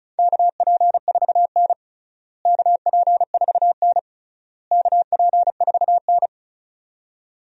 Morse code, heard as a keyed tone, about 34 wpm, 700 Hz. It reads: KP4N KP4N KP4N
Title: DXCC-100 Call Signs - 4 Characters - Repeated 3x - 35wpm